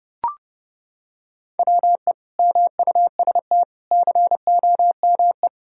Morse code, heard as a keyed tone, about 29 wpm, 700 Hz. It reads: WE MUST COME
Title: Sentences from Top 1000 Words - Review Part 1 - 30wpm